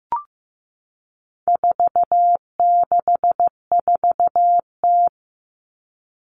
Morse code, heard as a keyed tone, about 15 wpm, 700 Hz. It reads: 464T
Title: All Cut Numbers - 4-Digit 15wpm